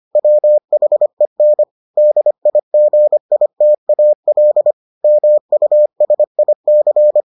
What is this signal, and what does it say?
Morse code, heard as a keyed tone, about 25 wpm, 600 Hz. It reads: WHEN DIGITAL MUSIC